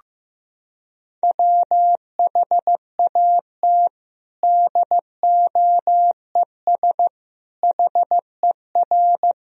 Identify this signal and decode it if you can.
Morse code, heard as a keyed tone, about 15 wpm, 700 Hz. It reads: WHAT DOES HER
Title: Sentences from Top 1000 Words - Review Part 4 - 15wpm